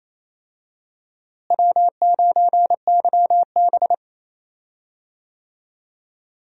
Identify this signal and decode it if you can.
Morse code, heard as a keyed tone, about 28 wpm, 700 Hz. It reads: W9YB
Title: USA Call Signs - 4 Characters - Repeated 2x - 28wpm